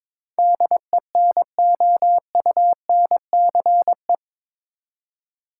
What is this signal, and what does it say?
Morse code, heard as a keyed tone, about 22 wpm, 700 Hz. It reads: DENOUNCE
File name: practice / 22wpm / Eight Letter Words - 22wpm